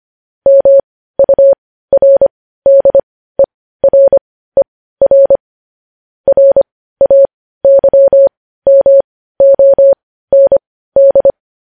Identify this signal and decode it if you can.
Morse code, heard as a keyed tone, about 25 wpm, spaced as for 15 wpm, 550 Hz. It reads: MURDERER RAYMOND
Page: News headlines at 15